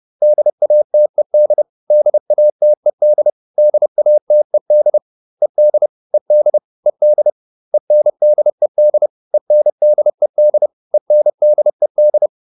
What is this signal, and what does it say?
Morse code, heard as a keyed tone, about 30 wpm, 600 Hz. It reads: DATED DATED DATED ED ED ED ENDED ENDED ENDED